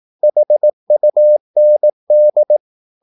Morse code, heard as a keyed tone, about 18 wpm, 600 Hz. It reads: HUND
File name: Korte ord DK 18wpm